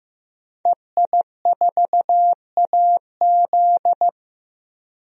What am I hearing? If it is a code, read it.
Morse code, heard as a keyed tone, about 15 wpm, 700 Hz. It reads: EI4AZ
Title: DXCC-100 Call Signs - 5 Characters - Repeated 2x - 15wpm